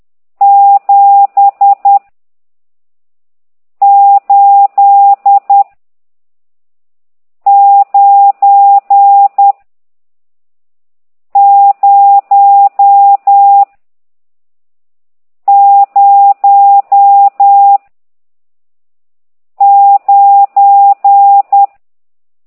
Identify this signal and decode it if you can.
Morse code, heard as a keyed tone, about 10 wpm, 800 Hz. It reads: 7 8 9 0 0 9
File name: ct2idw / exerc14